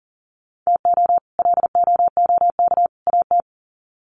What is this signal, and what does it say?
Morse code, heard as a keyed tone, about 40 wpm, 700 Hz. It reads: TO LOOK AT
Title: Sets of 3 Words 40wpm